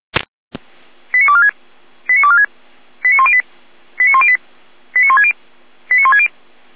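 Five-tone selective call (ZVEI), heard as quick runs of five tones, tones of about 70 ms. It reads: 89246 89246 89108 89108 89160 89160
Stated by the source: ZVEI five-tone selective call 89078 received by SDR.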